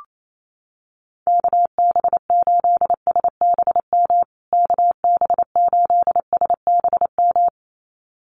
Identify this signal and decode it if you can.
Morse code, heard as a keyed tone, about 28 wpm, 700 Hz. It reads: KB8SBM KB8SBM
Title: USA Call Signs Repeated 2x 28wpm